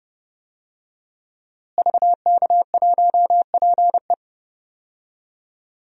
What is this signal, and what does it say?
Morse code, heard as a keyed tone, about 30 wpm, 700 Hz. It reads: VK1PE
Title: DXCC-100 Call Signs - 5 Characters - Repeated 3x - 30wpm